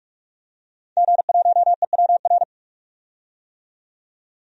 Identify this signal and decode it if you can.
Morse code, heard as a keyed tone, about 46 wpm, 700 Hz. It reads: G1EWR
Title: DXCC-100 Call Signs - 5 Characters - 45wpm